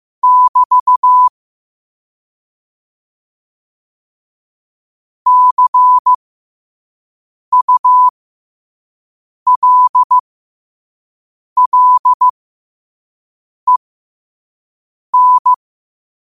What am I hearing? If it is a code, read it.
Morse code, heard as a keyed tone, about 15 wpm, spaced as for 5 wpm, 1000 Hz. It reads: <BT> CULLEN
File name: qotdfiles / 5wpm